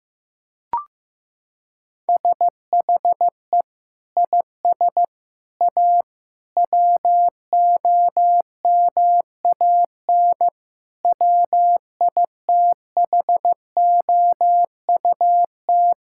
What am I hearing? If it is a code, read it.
Morse code, heard as a keyed tone, about 15 wpm, 700 Hz. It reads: SHE IS A WOMAN WITHOUT
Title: Sentences from Top 1600 Words 15wpm